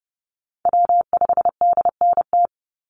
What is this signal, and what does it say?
Morse code, heard as a keyed tone, about 30 wpm, 700 Hz. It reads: W5DNT
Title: Call Signs Repeated 3x - 30wpm